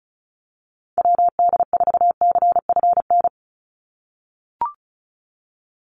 Morse code, heard as a keyed tone, about 34 wpm, 700 Hz. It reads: WD4CFN
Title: NA SOTA Chaser Callsigns 35wpm